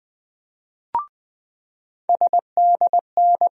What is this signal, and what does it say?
Morse code, heard as a keyed tone, about 20 wpm, 700 Hz. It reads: SDN
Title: USA Call Sign Postfixes - 3 Letters 20wpm